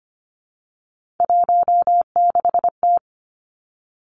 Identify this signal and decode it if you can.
Morse code, heard as a keyed tone, about 25 wpm, 700 Hz. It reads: 16T
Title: Three Challenging Letter-Number Combinations 25wpm